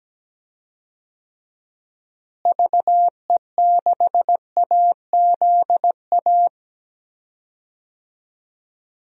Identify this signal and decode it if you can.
Morse code, heard as a keyed tone, about 17 wpm, 700 Hz. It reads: VE6AZA
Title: DXCC-100 Call Signs - 6 Characters - Repeated 3x - 17wpm